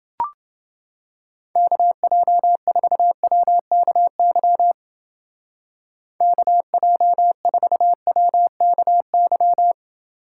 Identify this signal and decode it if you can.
Morse code, heard as a keyed tone, about 29 wpm, 700 Hz. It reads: KJ4WKY KJ4WKY